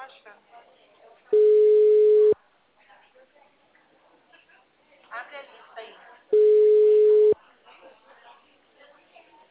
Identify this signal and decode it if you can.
Ringback tone (Europe, 425 Hz), with speech.
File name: out- -2022- - - .50